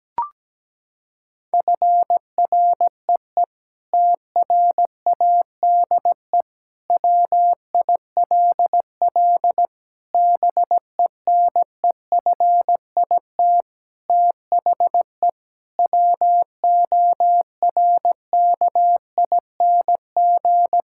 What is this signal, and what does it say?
Morse code, heard as a keyed tone, about 17 wpm, 700 Hz. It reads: FREE TRADE WILL BENEFIT THE WORKING
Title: Sentences from Top 1000 Words - Review Part 6 - 17wpm